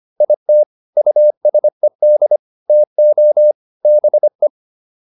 Morse code, heard as a keyed tone, about 25 wpm, 600 Hz. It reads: IT USED TO BE